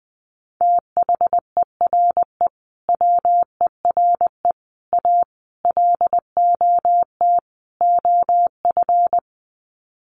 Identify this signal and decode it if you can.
Morse code, heard as a keyed tone, about 20 wpm, 700 Hz. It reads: THERE WERE A LOT OF
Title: Sets of 5 Words 20wpm